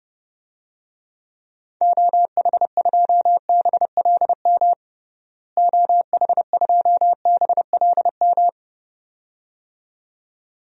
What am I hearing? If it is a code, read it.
Morse code, heard as a keyed tone, about 30 wpm, 700 Hz. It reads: OH2BLM OH2BLM
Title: DXCC-100 Call Signs - 6 Characters - Repeated 2x - 30wpm